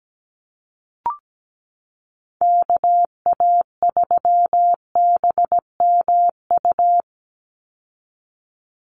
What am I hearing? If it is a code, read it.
Morse code, heard as a keyed tone, about 17 wpm, 700 Hz. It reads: KA3BMU
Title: Call Signs Repeated 2x - 17wpm